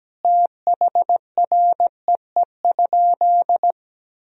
Morse code, heard as a keyed tone, about 17 wpm, 700 Hz. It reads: THREE?
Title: Sentences from Top 1000 Words - Review Part 2 - 17wpm